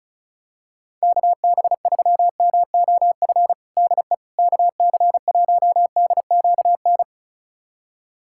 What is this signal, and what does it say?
Morse code, heard as a keyed tone, about 36 wpm, 700 Hz. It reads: KB3MOF DE KC1DQN
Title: USA Callsign DE Callsign 35wpm